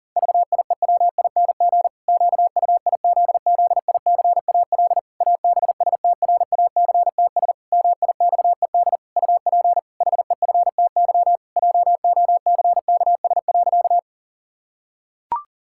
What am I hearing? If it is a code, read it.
Morse code, heard as a keyed tone, about 40 wpm, 700 Hz. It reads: VIEWING QUIZZICAL ABSTRACTS MIXED UP HEFTY JOCKS.
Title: Pangram Challenge v2 - 40wpm